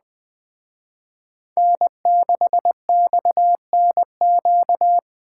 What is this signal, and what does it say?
Morse code, heard as a keyed tone, about 20 wpm, 700 Hz. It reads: N6XNQ
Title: Speed Racing - USA Call Signs 17wpm